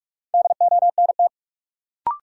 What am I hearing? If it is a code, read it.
Morse code, heard as a keyed tone, about 46 wpm, 700 Hz. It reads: DONT
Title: Sentences from Top 300 Words - New Word Drill - 45wpm